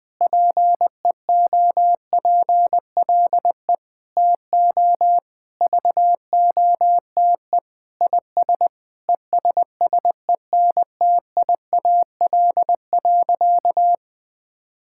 Morse code, heard as a keyed tone, about 21 wpm, 700 Hz. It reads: PEOPLE TO VOTE IS ESSENTIAL.
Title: Sentences with 6-Word Phrases - Spoken Beginning - 20wpm